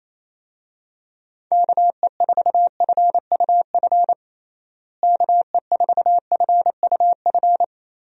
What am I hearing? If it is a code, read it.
Morse code, heard as a keyed tone, about 28 wpm, 700 Hz. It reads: KE4FUF KE4FUF
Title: USA Call Signs - 6 Characters - Repeated 2x - 28wpm